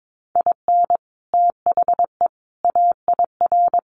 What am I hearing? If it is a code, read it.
Morse code, heard as a keyed tone, about 22 wpm, 700 Hz. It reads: IN THE AIR
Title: Sets of 3 Words 22wpm